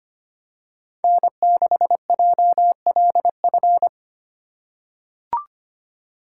Morse code, heard as a keyed tone, about 25 wpm, 700 Hz. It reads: N6JLF
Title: USA Call Signs - 5 Characters - 2x Character Spacing - 25wpm